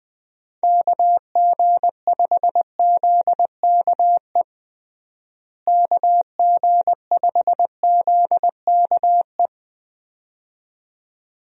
Morse code, heard as a keyed tone, about 21 wpm, 700 Hz. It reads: KG5ZKE KG5ZKE
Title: USA Call Signs - 6 Characters - Repeated 2x - 20wpm